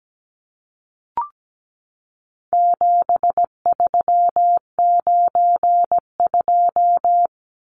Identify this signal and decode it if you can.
Morse code, heard as a keyed tone, about 17 wpm, 700 Hz. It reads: 7392
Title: Numbers - 4-Digit 17wpm